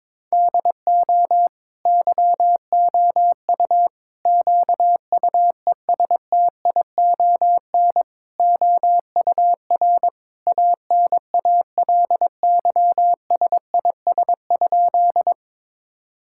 Morse code, heard as a keyed tone, about 22 wpm, 700 Hz. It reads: DO YOU QUESTION OUR ANALYSIS?
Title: Sentences from Top 1000 Words - Review Part 5 - 22wpm